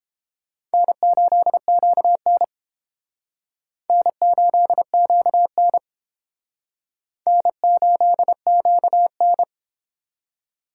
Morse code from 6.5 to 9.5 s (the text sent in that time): N8QN